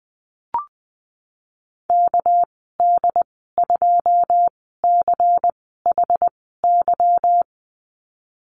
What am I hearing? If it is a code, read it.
Morse code, heard as a keyed tone, about 20 wpm, 700 Hz. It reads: K D 2 C H Y
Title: 238 Lesson 38g - Letter q - Call Signs 20wpm